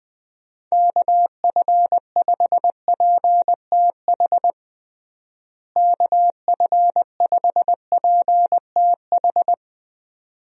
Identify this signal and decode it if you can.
Morse code, heard as a keyed tone, about 20 wpm, 700 Hz. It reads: KF5PTH KF5PTH
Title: USA Call Signs - 6 Characters - Repeated 2x - 20wpm